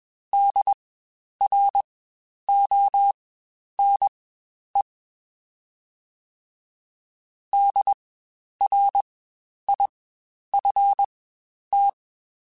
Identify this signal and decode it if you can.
Morse code, heard as a keyed tone, about 20 wpm, spaced as for 10 wpm, 800 Hz. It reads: DRONE DRIFT
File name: Spaced Alphas / Spaced Alphas mp3s / Ds sp @20